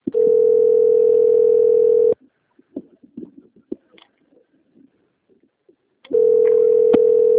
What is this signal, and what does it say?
Ringback tone (North America), 440 + 480 Hz.